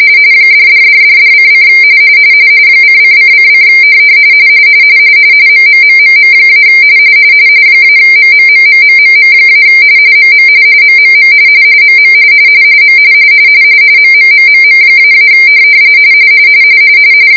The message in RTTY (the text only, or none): $$$$$:: ::::: MMMMM- -MMMMMMM- MM MMM MM M MMMMM--